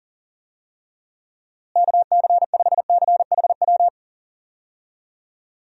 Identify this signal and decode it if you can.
Morse code, heard as a keyed tone, about 40 wpm, 700 Hz. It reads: KC5CHW
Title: USA Call Signs - 6 Characters - Repeated 3x - 40wpm